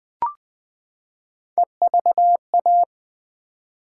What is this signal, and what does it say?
Morse code, heard as a keyed tone, about 21 wpm, 700 Hz. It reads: EVA
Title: Build A Word - 3 Letters At A Time - 8-Letter Words - 20wpm